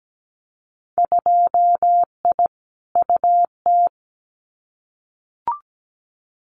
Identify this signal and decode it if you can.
Morse code, heard as a keyed tone, about 17 wpm, 700 Hz. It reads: 2I UT
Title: Winter Field Day Exchanges 17wpm